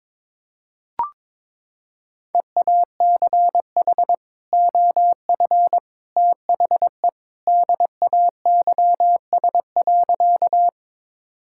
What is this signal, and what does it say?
Morse code, heard as a keyed tone, about 22 wpm, 700 Hz. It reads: EACH OF THE DAYS.
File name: practice / 22wpm / Sentences with 3-Word Phrases - 22wpm